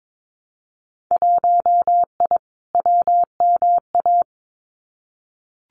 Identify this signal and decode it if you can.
Morse code, heard as a keyed tone, about 22 wpm, 700 Hz. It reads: 1I WMA
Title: Winter Field Day Exchanges Encore 22wpm